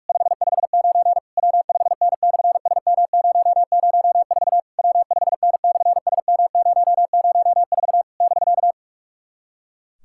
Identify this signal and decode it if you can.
Morse code, heard as a keyed tone, about 46 wpm, 700 Hz. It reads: 559 W5N/SM004 W5N/SM004 <BK>
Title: SOTA QSO Practice 45wpm